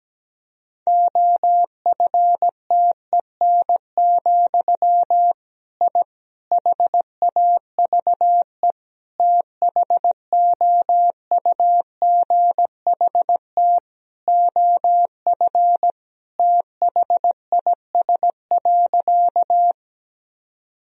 Morse code, heard as a keyed tone, about 17 wpm, 700 Hz. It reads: OFTEN, I HAVE THOUGHT OF THIS.
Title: Sentences from Top 1000 Words - Review Part 1 - 17wpm